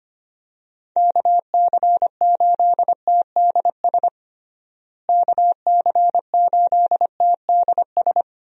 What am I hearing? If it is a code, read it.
Morse code, heard as a keyed tone, about 25 wpm, 700 Hz. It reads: KC8TDS KC8TDS
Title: USA Call Signs - 6 Characters - Repeated 3x - 25wpm